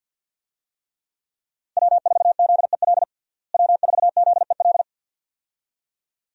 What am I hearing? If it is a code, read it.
Morse code, heard as a keyed tone, about 52 wpm, 700 Hz. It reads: W4ZEL W4ZEL
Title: USA Call Signs - 5 Characters - Repeated 2x - 50wpm